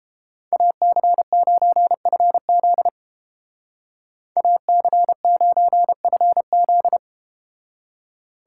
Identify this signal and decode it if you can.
Morse code, heard as a keyed tone, about 30 wpm, 700 Hz. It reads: AC9FZ AC9FZ